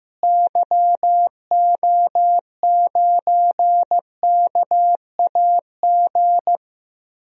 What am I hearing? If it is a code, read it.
Morse code, heard as a keyed tone, about 16 wpm, 700 Hz. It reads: YO9KAG
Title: DXCC-100 Call Signs - 6 Characters - Repeated 3x - 15wpm